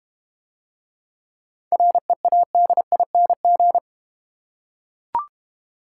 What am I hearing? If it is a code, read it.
Morse code, heard as a keyed tone, about 31 wpm, 700 Hz. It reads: READING
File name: Easy Words - Speed-Racing with Hints - 25wpm